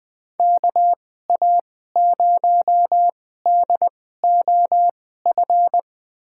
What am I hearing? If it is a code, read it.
Morse code, heard as a keyed tone, about 21 wpm, spaced as for 15 wpm, 700 Hz. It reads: KA0DOF
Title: USA Call Signs - 6 Characters - 2x Character Spacing - 20wpm